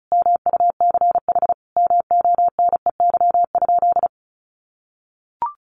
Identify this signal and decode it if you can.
Morse code, heard as a keyed tone, about 34 wpm, 700 Hz. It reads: MUCH MONEY?
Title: Sentences from Top 1400 Words 35wpm